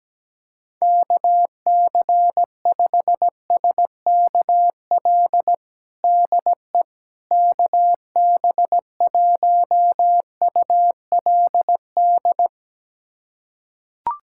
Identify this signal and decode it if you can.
Morse code, heard as a keyed tone, about 17 wpm, 700 Hz. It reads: KC5SKL DE KB1ULD
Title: USA Callsign DE Callsign - 6 Characters - 17wpm